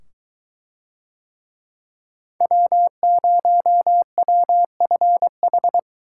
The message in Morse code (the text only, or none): W0WFH